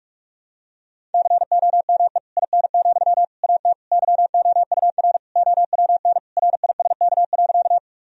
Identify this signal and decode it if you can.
Morse code, heard as a keyed tone, about 44 wpm, 700 Hz. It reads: COME IN, AT YOUR OWN RISK.
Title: Sentences from Top 1000 Words - Review Part 4 - 45wpm